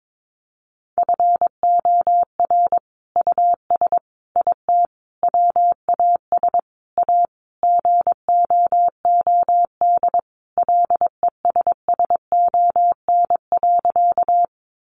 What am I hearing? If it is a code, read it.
Morse code, heard as a keyed tone, about 22 wpm, 700 Hz. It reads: FOR US IT WAS A GOOD LESSON.
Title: Sentences from Top 1200 Words 22wpm